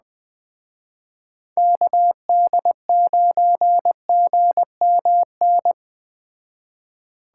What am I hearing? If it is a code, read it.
Morse code, heard as a keyed tone, about 21 wpm, 700 Hz. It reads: KD9GMN
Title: Speed Racing - USA 2x3 Call Signs 17wpm